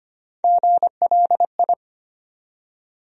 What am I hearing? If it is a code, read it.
Morse code, heard as a keyed tone, about 25 wpm, 700 Hz. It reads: GLI
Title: USA Call Sign Postfixes - 3 Letters 25wpm